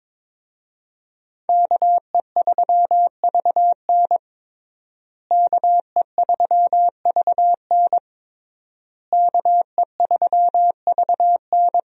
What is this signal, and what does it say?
Morse code, heard as a keyed tone, about 21 wpm, 700 Hz. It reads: KE3VN KE3VN KE3VN
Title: USA Call Signs - 5 Characters - Repeated 3x - 22wpm